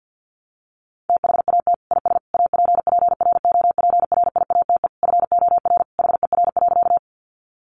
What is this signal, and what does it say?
Morse code, heard as a keyed tone, about 50 wpm, 700 Hz. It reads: THAT IS APPROPRIATE FOR HER.
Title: Sentences from Top 900 Words 50wpm